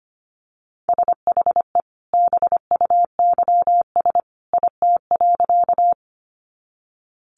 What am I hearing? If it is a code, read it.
Morse code, heard as a keyed tone, about 25 wpm, 700 Hz. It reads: SHE BUYS IT.
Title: Additional Easy Phrases 25wpm